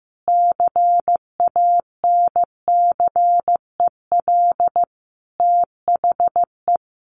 Morse code, heard as a keyed tone, about 15 wpm, 700 Hz. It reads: CANCEL THE
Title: Sentences from Top 1200 Words 15wpm